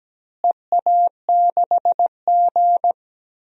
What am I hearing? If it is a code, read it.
Morse code, heard as a keyed tone, about 17 wpm, 700 Hz. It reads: EA6G
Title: DXCC-100 Call Signs - 4 Characters - Repeated 2x - 17wpm